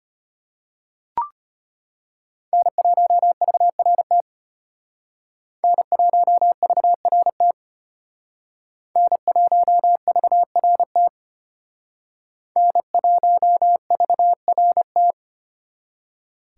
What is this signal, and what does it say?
Morse code, heard as a keyed tone, about 32 wpm, 700 Hz. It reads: N1VRT N1VRT N1VRT N1VRT